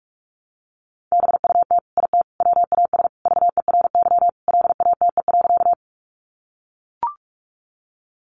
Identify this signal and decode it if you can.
Morse code, heard as a keyed tone, about 44 wpm, 700 Hz. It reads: BUT IT WAS VERY LATE.
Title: Sentences from Top 300 Words 45wpm